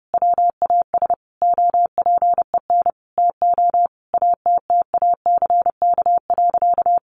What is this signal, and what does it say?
Morse code, heard as a keyed tone, about 30 wpm, 700 Hz. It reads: WAS OPEN TO ATTACK.
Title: Sentences from Top 1500 Words 30wpm